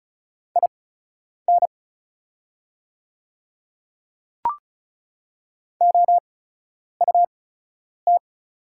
Morse code, heard as a keyed tone, about 34 wpm, spaced as for 10 wpm, 690 Hz. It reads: IN OUT
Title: 8x Character Spacing - Top 100 Words - 35wpm